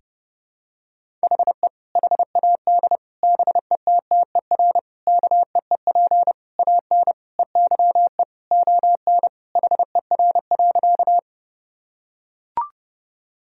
Morse code, heard as a keyed tone, about 32 wpm, 700 Hz. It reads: HE HAD BETTER KEEP AN EYE ON HER.
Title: Sentences from Top 1000 Words - Review Part 1 - 30wpm